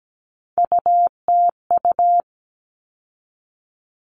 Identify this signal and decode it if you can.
Morse code, heard as a keyed tone, about 17 wpm, 700 Hz. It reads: UTU
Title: All Cut Numbers - 3-Digit 17wpm